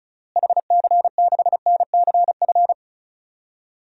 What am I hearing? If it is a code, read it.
Morse code, heard as a keyed tone, about 34 wpm, 700 Hz. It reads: HC6NCF
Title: DXCC-100 Call Signs - 6 Characters - Repeated 3x - 35wpm